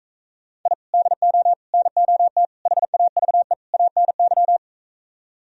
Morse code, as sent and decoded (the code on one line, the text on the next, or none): .. / -.. --- / -. --- - / .... .- ...- . / .- -. -.--
I DO NOT HAVE ANY